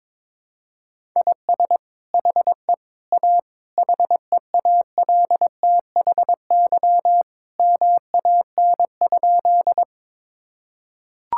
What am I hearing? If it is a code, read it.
Morse code, heard as a keyed tone, about 22 wpm, 700 Hz. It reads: IS HE A HEALTHY MAN?
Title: Sentences from Top 1000 Words - Review Part 5 - 22wpm